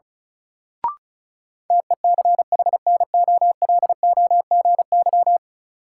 Morse code, heard as a keyed tone, about 36 wpm, 700 Hz. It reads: TECHNOLOGY
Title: Sentences from Top 1200 Words - New Word Drill Alt - 35wpm